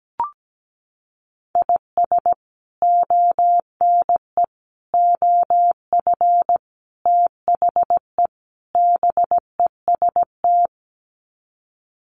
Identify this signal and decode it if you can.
Morse code, heard as a keyed tone, about 17 wpm, 700 Hz. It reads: IS ONE OF THE BEST
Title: Sets of 5 Words 17wpm